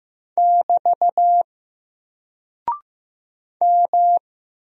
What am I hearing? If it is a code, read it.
Morse code, heard as a keyed tone, about 15 wpm, 700 Hz. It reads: <BT> M
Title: Single Character (Letter:Number:Punctuation:Prosign) Alt 15wpm